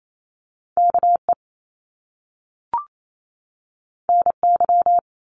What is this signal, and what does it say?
Morse code, heard as a keyed tone, about 28 wpm, 700 Hz. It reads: KE NY
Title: Top Two Letter Combinations in Words 28wpm